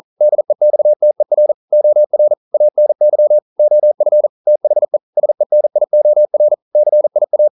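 Morse code, heard as a keyed tone, about 41 wpm, 600 Hz. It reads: DEXTER OR ANY OF THE SENIOR CIA
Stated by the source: Morse code practice at 41 wpm, from CW med Gnister.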